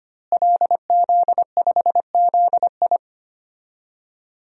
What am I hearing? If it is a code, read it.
Morse code, heard as a keyed tone, about 25 wpm, 700 Hz. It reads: LZ5ZI
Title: DXCC-100 Call Signs - 5 Characters - Repeated 2x - 25wpm